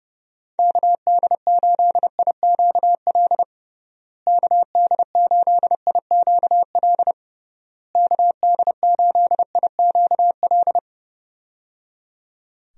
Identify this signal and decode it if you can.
Morse code, heard as a keyed tone, about 30 wpm, 700 Hz. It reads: KD8IQL KD8IQL KD8IQL
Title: USA Call Signs - 6 Characters - Repeated 3x - 30wpm